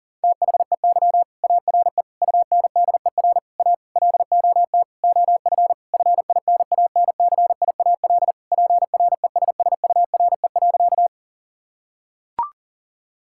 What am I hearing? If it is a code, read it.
Morse code, heard as a keyed tone, about 40 wpm, 700 Hz. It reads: THEY ARE UNDER A LOT OF FINANCIAL PRESSURE.
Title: Sentences from Top 1000 Words - Review Part 6 - 40wpm